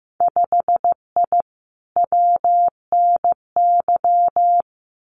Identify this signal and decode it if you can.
Morse code, heard as a keyed tone, about 15 wpm, 700 Hz. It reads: 5I WNY
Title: Winter Field Day Exchanges Encore 15wpm